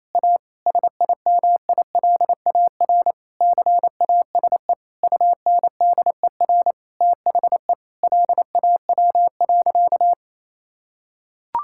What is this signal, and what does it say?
Morse code, heard as a keyed tone, about 28 wpm, 700 Hz. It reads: A SIMILAR CASE UNDER THE LAW.
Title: Sentences from Top 1000 Words - Review Part 5 - 28wpm